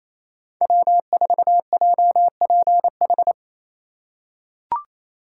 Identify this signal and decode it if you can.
Morse code, heard as a keyed tone, about 28 wpm, 700 Hz. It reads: W4JPH
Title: K1USN SST QSO Practice - 28wpm